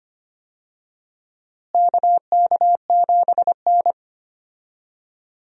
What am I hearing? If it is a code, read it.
Morse code, heard as a keyed tone, about 25 wpm, 700 Hz. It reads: KK7N
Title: USA Call Signs - 4 Characters - Repeated 3x - 25wpm